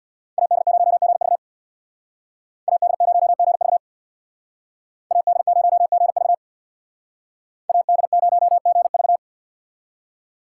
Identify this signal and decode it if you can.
Morse code, heard as a keyed tone, about 61 wpm, 700 Hz. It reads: AD0GV AD0GV AD0GV AD0GV
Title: Speed Racing - USA 2x2 Call Signs 50wpm